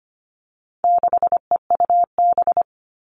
Morse code, heard as a keyed tone, about 25 wpm, 700 Hz. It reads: 6EUB
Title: All Cut Numbers - 4-Digit 25wpm